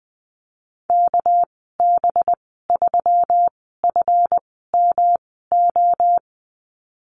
Morse code, heard as a keyed tone, about 20 wpm, spaced as for 15 wpm, 700 Hz. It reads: KB3FMO
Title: 200 Lesson 32g - Letter k - Call Signs 20wpm